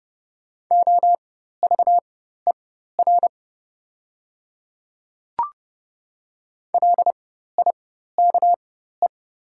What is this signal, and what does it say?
Morse code, heard as a keyed tone, about 30 wpm, spaced as for 14 wpm, 700 Hz. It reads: OVER LIKE
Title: 4x Character Spacing - Top 100 Words - 30wpm